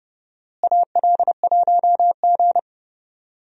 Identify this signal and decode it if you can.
Morse code, heard as a keyed tone, about 30 wpm, 700 Hz. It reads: AL1G
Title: USA Call Signs - 4 Characters - Repeated 2x - 30wpm